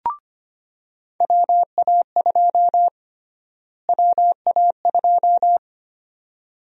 Morse code, heard as a keyed tone, about 25 wpm, 700 Hz. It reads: WA2 WA2
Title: USA Call Sign Prefixes - Repeated 2x - 25wpm